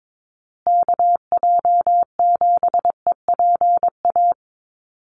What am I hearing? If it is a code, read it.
Morse code, heard as a keyed tone, about 22 wpm, 700 Hz. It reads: KJ7EPA
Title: USA Call Signs Repeated 2x 22wpm